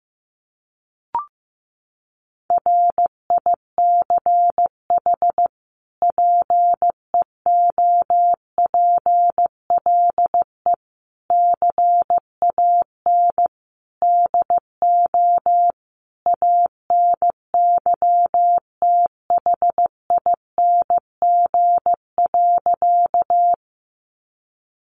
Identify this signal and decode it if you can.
Morse code, heard as a keyed tone, about 15 wpm, 700 Hz. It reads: RICH PEOPLE CAN DO ANYTHING.
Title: Sentences from Top 1100 Words 15wpm